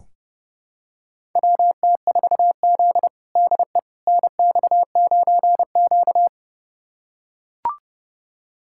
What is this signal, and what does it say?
Morse code, heard as a keyed tone, about 30 wpm, 700 Hz. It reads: WT4Z DE NX9Q
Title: USA Callsign DE Callsign - 4 Characters - 30wpm